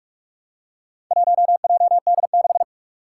Morse code, heard as a keyed tone, about 44 wpm, 700 Hz. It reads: 1JD6
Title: Four Challenging Letter-Number Combinations 45wpm